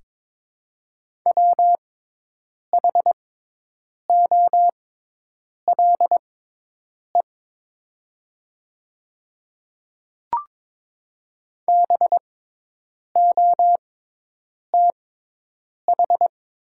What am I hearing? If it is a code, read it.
Morse code, heard as a keyed tone, about 22 wpm, spaced as for 8 wpm, 700 Hz. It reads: WHOLE BOTH
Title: 6x Character Spacing - Top 500 Words - 22wpm